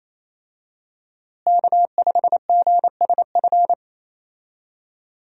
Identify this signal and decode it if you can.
Morse code, heard as a keyed tone, about 27 wpm, 700 Hz. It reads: K5GSF
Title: USA Call Signs - 5 Characters - Repeated 3x - 28wpm